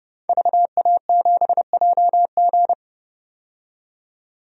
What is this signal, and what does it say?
Morse code, heard as a keyed tone, about 31 wpm, 700 Hz. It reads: VA7JG
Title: DXCC-100 Call Signs - 5 Characters - 30wpm